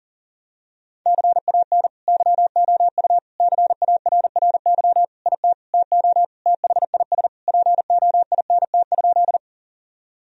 Morse code, heard as a keyed tone, about 40 wpm, 700 Hz. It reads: CAN YOU CARRY IT TO THIS POINT?
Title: Sentences from Top 1000 Words - Review Part 1 - 40wpm